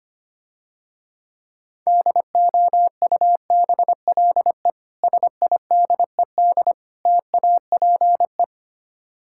Morse code, heard as a keyed tone, about 26 wpm, 700 Hz. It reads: DOUBLE SIDED TAPE
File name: Easy Words - Speed-Racing with Hints - 25wpm